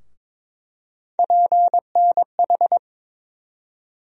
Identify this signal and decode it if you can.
Morse code, heard as a keyed tone, about 22 wpm, 700 Hz. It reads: PNH
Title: USA Call Sign Postfixes - 3 Letters 22wpm